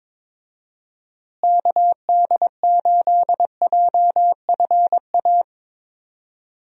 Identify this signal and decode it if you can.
Morse code, heard as a keyed tone, about 22 wpm, 700 Hz. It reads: KD8JFA
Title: USA Call Signs - 6 Characters - Repeated 3x - 22wpm